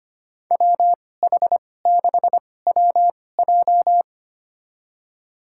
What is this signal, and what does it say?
Morse code, heard as a keyed tone, about 25 wpm, spaced as for 18 wpm, 700 Hz. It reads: WH6WJ
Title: USA Call Signs - 2x Character Spacing - 25wpm